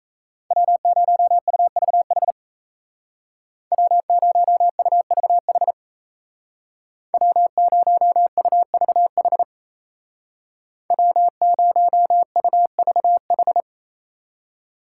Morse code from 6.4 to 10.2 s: W0UVH

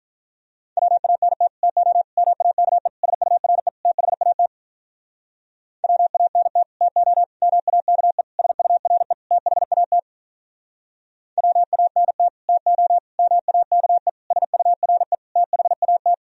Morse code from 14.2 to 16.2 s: SURE THAT

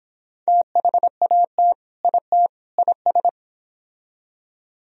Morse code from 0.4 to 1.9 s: THAT

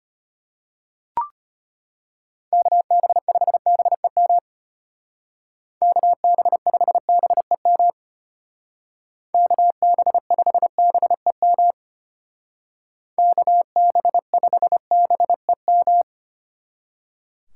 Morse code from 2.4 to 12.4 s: KB5BEM KB5BEM KB5BEM

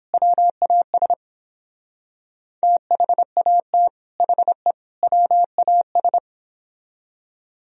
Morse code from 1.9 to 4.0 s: THAT

